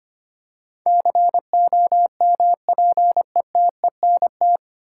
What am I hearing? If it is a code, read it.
Morse code, heard as a keyed tone, about 25 wpm, 700 Hz. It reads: COMPETENT
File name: practice / 25wpm / 9 Letter Words - First 3 Letters Given - 25wpm